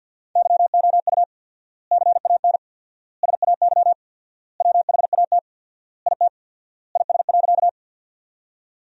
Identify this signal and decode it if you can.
Morse code, heard as a keyed tone, about 50 wpm, 700 Hz. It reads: YOU CAN SAY WHAT IT IS.
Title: 4x Word-Spacing - Sentences from Top 200 Words - 50wpm